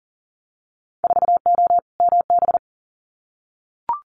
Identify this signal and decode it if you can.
Morse code, heard as a keyed tone, about 40 wpm, 700 Hz. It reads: 4O MB
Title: Winter Field Day Exchanges Encore 40wpm